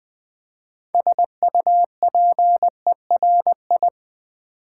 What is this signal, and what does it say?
Morse code, heard as a keyed tone, about 20 wpm, 700 Hz. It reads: SUPERI
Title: Build A Word - 3 Letters At A Time - 8-Letter Words - 20wpm